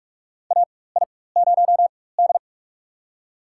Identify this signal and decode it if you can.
Morse code, heard as a keyed tone, about 44 wpm, spaced as for 21 wpm, 700 Hz. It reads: AI0D
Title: USA Call Signs - 4 Characters - 4x Character Spacing - 45wpm